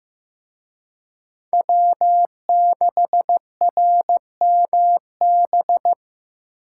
Morse code, heard as a keyed tone, about 18 wpm, 700 Hz. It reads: W6RMB
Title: Speed Racing - USA 1x3 Call Signs 15wpm